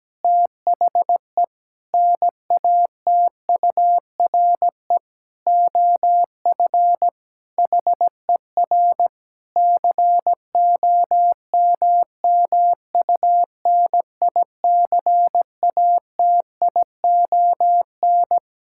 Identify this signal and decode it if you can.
Morse code, heard as a keyed tone, about 17 wpm, 700 Hz. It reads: THE NATURE OF HER COMMUNICATION
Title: Sentences from Top 1000 Words - Review Part 6 - 17wpm